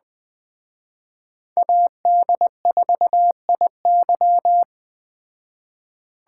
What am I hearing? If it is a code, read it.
Morse code, heard as a keyed tone, about 20 wpm, 700 Hz. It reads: AD4IY
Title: Speed Racing - USA 2x2 Call Signs 17wpm